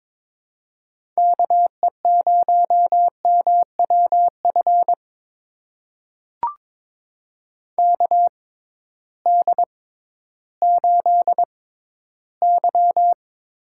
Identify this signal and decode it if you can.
Morse code, heard as a keyed tone, about 22 wpm, spaced as for 13 wpm, 700 Hz. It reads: KE0MWF KD8Y
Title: USA Call Signs - 6 Characters - 6x Character Spacing - 22wpm